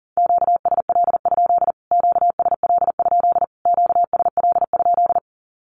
Morse code, heard as a keyed tone, about 40 wpm, 700 Hz. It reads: QSL? QSL? QSL?
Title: Q-Codes Repeated 3x 40wpm